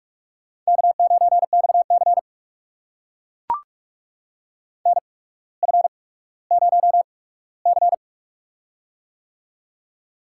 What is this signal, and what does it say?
Morse code, heard as a keyed tone, about 44 wpm, spaced as for 19 wpm, 690 Hz. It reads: K9XC NF0C
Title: USA Call Signs - 4 Characters - 8x Character Spacing - 45wpm